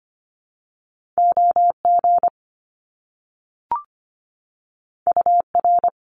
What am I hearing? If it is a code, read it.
Morse code, heard as a keyed tone, about 25 wpm, 700 Hz. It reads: OG UR
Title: Top Two Letter Combinations in Words 25wpm